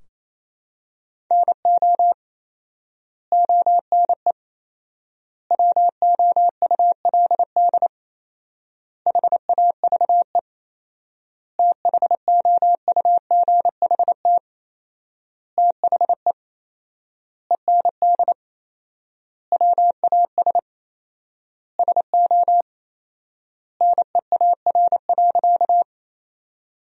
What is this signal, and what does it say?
Morse code, heard as a keyed tone, about 27 wpm, 700 Hz. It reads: NO ONE WOULD HAVE THOUGHT THE END WAS SO NEAR.
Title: 4x Word-Spacing - Sentences from Top 300 Words - 28wpm